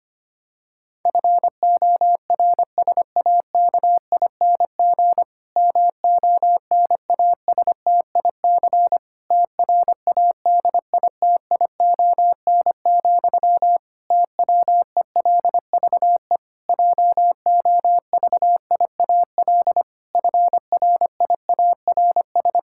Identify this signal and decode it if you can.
Morse code, heard as a keyed tone, about 25 wpm, 700 Hz. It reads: FORSAKING MONASTIC TRADITION, TWELVE JOVIAL FRIARS
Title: Pangram Challenge v2 - 25wpm